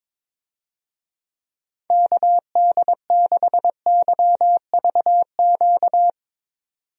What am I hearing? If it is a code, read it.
Morse code, heard as a keyed tone, about 22 wpm, 700 Hz. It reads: KD6YVQ
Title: USA Call Signs - 6 Characters - 22wpm